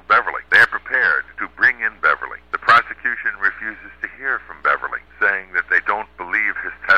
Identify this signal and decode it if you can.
Emergency Alert System (SAME) tones (header bursts), heard as harsh buzzing bursts, with speech. It is heard from 0.9 to 5.8 s.